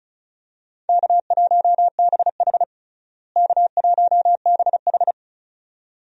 Morse code, heard as a keyed tone, about 36 wpm, 700 Hz. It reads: K1BH K1BH
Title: USA Call Signs - 4 Characters - Repeated 2x - 35wpm